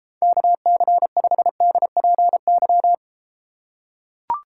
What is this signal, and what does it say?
Morse code, heard as a keyed tone, about 33 wpm, 700 Hz. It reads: KC5DPY